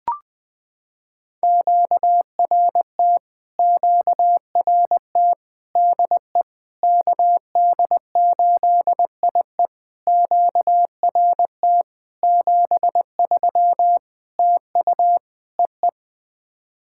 Morse code, heard as a keyed tone, about 20 wpm, 700 Hz. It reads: QRT QRT DE KD8IE QRT 73 TU EE
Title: POTA QSO Practice - 20wpm